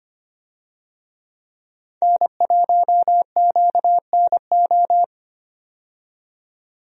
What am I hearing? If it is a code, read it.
Morse code, heard as a keyed tone, about 25 wpm, 700 Hz. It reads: N1QNO
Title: USA Call Signs - 5 Characters - 25wpm